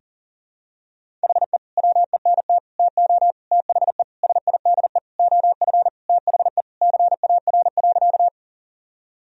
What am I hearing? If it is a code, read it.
Morse code, heard as a keyed tone, about 40 wpm, 700 Hz. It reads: HE WENT TO THE SIDE OF THE CAR.
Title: Sentences from Top 1000 Words - Review Part 1 - 40wpm